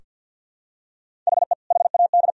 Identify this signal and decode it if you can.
Morse code, heard as a keyed tone, about 46 wpm, 700 Hz. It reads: HE HAD
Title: Sentences from Top 1000 Words - Review Part 4 - 50wpm